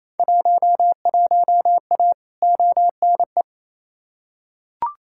Morse code, heard as a keyed tone, about 28 wpm, 700 Hz. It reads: 11A ONE
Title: Field Day Exchanges Encore - 28wpm